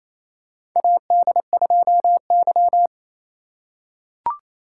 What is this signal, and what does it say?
Morse code, heard as a keyed tone, about 27 wpm, 700 Hz. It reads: AD2Y
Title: USA Call Signs - 4x Character Spacing - 28wpm